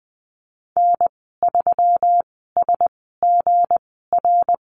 Morse code, heard as a keyed tone, about 20 wpm, spaced as for 15 wpm, 700 Hz. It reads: N3SGR
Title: 173 Lesson 28g - Number 9 - Call Signs 20wpm